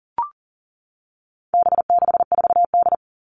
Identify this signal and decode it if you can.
Morse code, heard as a keyed tone, about 40 wpm, 700 Hz. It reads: B64D
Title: All Cut Numbers - 4-Digit 40wpm